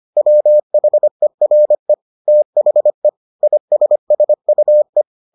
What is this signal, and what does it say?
Morse code, heard as a keyed tone, about 25 wpm, 600 Hz. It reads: WHERE THE ISSUE